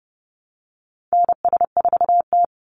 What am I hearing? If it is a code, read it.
Morse code, heard as a keyed tone, about 30 wpm, 700 Hz. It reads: NS4T
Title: Call Signs Repeated 3x - 30wpm